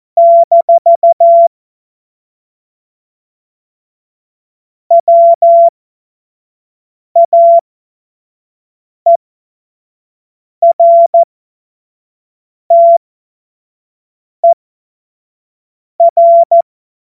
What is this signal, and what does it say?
Morse code, heard as a keyed tone, about 14 wpm, spaced as for 5 wpm, 680 Hz. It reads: - WAERTER